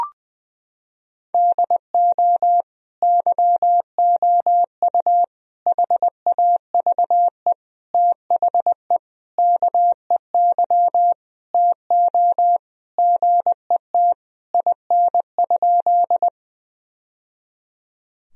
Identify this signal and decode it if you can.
Morse code, heard as a keyed tone, about 20 wpm, 700 Hz. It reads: DO YOU HAVE THE KEY TO GET IN?
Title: Sentences from Top 1000 Words - Review Part 4 - 20wpm